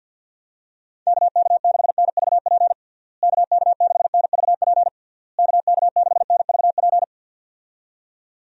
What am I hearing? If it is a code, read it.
Morse code, heard as a keyed tone, about 49 wpm, 700 Hz. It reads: KK6NVP KK6NVP KK6NVP
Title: USA Call Signs - 6 Characters - Repeated 3x - 50wpm